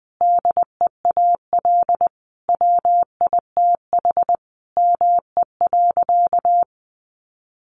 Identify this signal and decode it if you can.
Morse code, heard as a keyed tone, about 20 wpm, 700 Hz. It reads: DEAL WITH ME.
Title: Sentences from Top 800 Words 20wpm